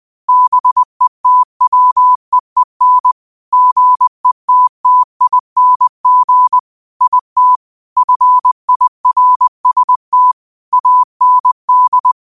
Morse code, heard as a keyed tone, about 20 wpm, 1000 Hz. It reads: BETWEEN GETTING IT FIRST AND
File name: qotdfiles / 20wpm